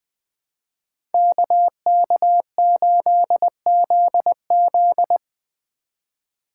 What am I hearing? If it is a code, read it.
Morse code, heard as a keyed tone, about 22 wpm, 700 Hz. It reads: KK8ZZ
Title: Speed Racing - USA 2x2 Call Signs 20wpm